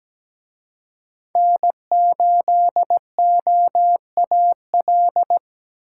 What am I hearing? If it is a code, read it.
Morse code, heard as a keyed tone, about 20 wpm, 700 Hz. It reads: N8OAL
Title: Speed Racing - USA 1x3 Call Signs 17wpm